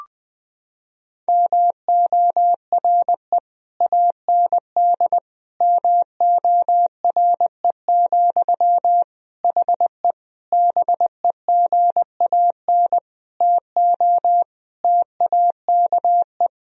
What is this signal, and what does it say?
Morse code, heard as a keyed tone, about 20 wpm, 700 Hz. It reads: MORE AND MORE, HE BEGAN TO TAKE
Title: Sentences from Top 1000 Words - Review Part 3 - 20wpm